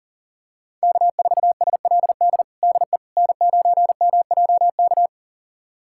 Morse code, heard as a keyed tone, about 37 wpm, 700 Hz. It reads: K4SLD DE N9MJK
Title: USA Callsign DE Callsign - 5 Characters - 40wpm